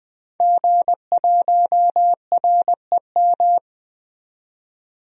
Morse code, heard as a keyed tone, about 21 wpm, 700 Hz. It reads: G1REM
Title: DXCC-100 Call Signs - 5 Characters - 20wpm